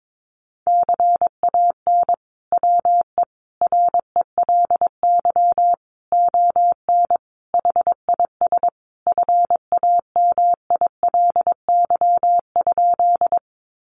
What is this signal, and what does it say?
Morse code, heard as a keyed tone, about 22 wpm, 700 Hz. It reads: CAN WE RELY ON HIS FAMILY?
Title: Sentences from Top 1200 Words 22wpm